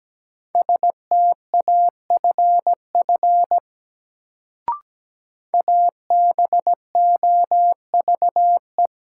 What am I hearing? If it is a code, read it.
Morse code, heard as a keyed tone, about 17 wpm, 700 Hz. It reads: STAFF ABOVE
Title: Sentences from Top 1200 Words - New Word Drill Alt - 17wpm